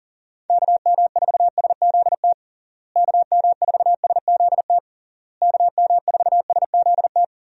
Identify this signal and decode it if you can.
Morse code, heard as a keyed tone, about 40 wpm, 700 Hz. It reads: KM4SZT KM4SZT KM4SZT
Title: USA Call Signs - 6 Characters - Repeated 3x - 40wpm